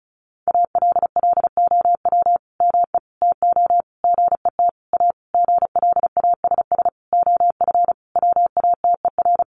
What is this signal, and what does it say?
Morse code, heard as a keyed tone, about 34 wpm, 700 Hz. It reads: ALLOW ME TO GET A GLASS OF WATER
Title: Sentences from Top 1300 Words 35wpm